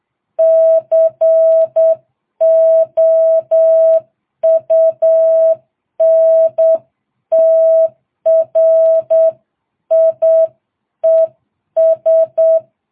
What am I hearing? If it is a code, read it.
Morse code, heard as a keyed tone, about 8 wpm, 650 Hz. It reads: COUNTRIES